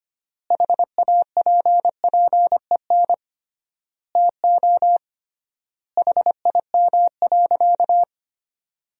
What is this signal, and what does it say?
Morse code, heard as a keyed tone, about 25 wpm, 700 Hz. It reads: HAPPEN TO HIM.
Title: 3x Word-Spacing - Sentences from Top 400 Words - 25wpm